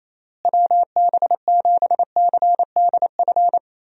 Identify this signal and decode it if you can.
Morse code, heard as a keyed tone, about 28 wpm, 700 Hz. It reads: WB7CDF
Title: USA Call Signs - 6 Characters - Repeated 3x - 28wpm